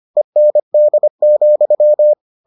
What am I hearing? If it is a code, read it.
Morse code, heard as a keyed tone, about 25 wpm, 600 Hz. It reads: END,